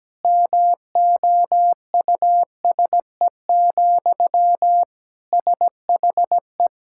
Morse code, heard as a keyed tone, about 17 wpm, 700 Hz. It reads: MOUSE, SHE
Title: Sentences from Top 1000 Words - Review Part 5 - 17wpm